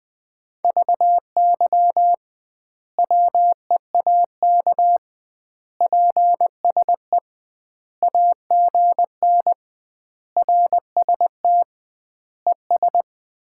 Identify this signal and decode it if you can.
Morse code, heard as a keyed tone, about 20 wpm, 700 Hz. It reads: VY WEAK PSE AGN RST ES
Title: Common QSO Phrases - 2x Word-Spacing - 20wpm